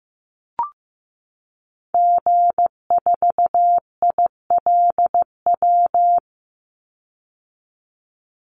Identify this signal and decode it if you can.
Morse code, heard as a keyed tone, about 15 wpm, 700 Hz. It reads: G4ILW
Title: Call Signs Repeated 2x - 15wpm